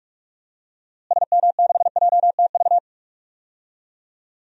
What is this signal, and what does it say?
Morse code, heard as a keyed tone, about 41 wpm, 700 Hz. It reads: SM6JTV
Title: DXCC-100 Call Signs - 6 Characters - Repeated 2x - 45wpm